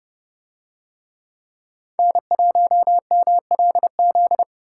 Morse code, heard as a keyed tone, about 29 wpm, 700 Hz. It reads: N1MLZ